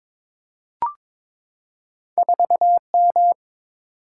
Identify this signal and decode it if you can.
Morse code, heard as a keyed tone, about 22 wpm, 700 Hz. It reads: 4M
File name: practice / 22wpm / Two Letter-Number Combinations - 22wpm